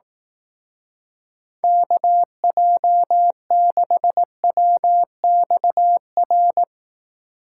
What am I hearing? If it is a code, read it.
Morse code, heard as a keyed tone, about 18 wpm, 700 Hz. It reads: KJ6WXR
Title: Speed Racing - USA 2x3 Call Signs 15wpm